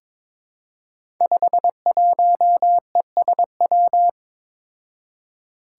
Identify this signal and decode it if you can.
Morse code, heard as a keyed tone, about 23 wpm, 700 Hz. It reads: 51ESW
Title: Five Challenging Letter-Number Combinations 22wpm